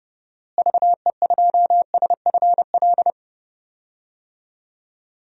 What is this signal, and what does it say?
Morse code, heard as a keyed tone, about 30 wpm, 700 Hz. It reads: VE2SFL
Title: DXCC-100 Call Signs - 6 Characters - Repeated 3x - 30wpm